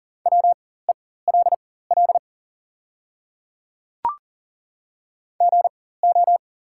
Morse code, heard as a keyed tone, about 42 wpm, spaced as for 19 wpm, 690 Hz. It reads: WELL GO
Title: 4x Character Spacing - Top 300 Words - 40wpm